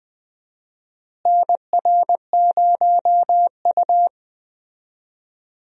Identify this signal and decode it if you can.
Morse code, heard as a keyed tone, about 20 wpm, 700 Hz. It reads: NR0U
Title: USA Call Signs - 4 Characters - Repeated 3x - 20wpm